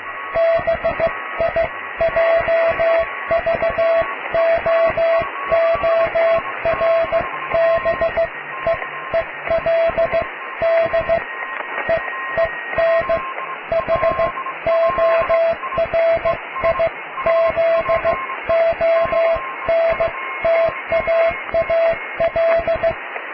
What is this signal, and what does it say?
Morse code, heard as a keyed tone, about 15 wpm, spaced as for 13 wpm, 660 Hz, with radio noise. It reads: BIJVOORBEELD EEN HORIZONTAAL
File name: cw / tekst(a)+QRM